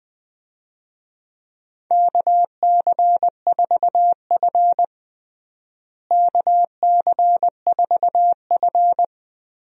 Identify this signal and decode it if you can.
Morse code, heard as a keyed tone, about 21 wpm, 700 Hz. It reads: KC4F KC4F
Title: USA Call Signs - 4 Characters - Repeated 2x - 20wpm